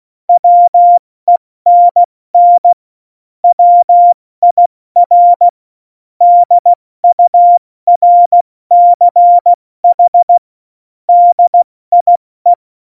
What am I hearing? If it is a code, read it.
Morse code, heard as a keyed tone, about 16 wpm, spaced as for 14 wpm, 700 Hz. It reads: WENN WIR DURCH DIE
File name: Morsekurs / Witze / Witz 06. November 2018, 70 F80